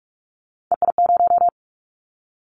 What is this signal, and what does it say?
Morse code, heard as a keyed tone, about 46 wpm, 700 Hz. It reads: EI0
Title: Callsign Prefixes for Top 50 DXCC - 45wpm